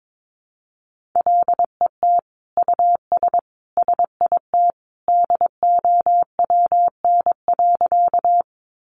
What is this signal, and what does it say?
Morse code, heard as a keyed tone, about 22 wpm, 700 Hz. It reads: LET US SIT DOWN.
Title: Sentences from Top 500 Words 22wpm